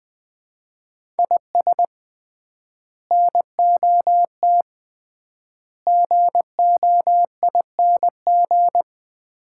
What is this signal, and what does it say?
Morse code, heard as a keyed tone, about 20 wpm, 700 Hz. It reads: IS NOT GOING
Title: 3x Word-Spacing - Sets of 4 Words - 20wpm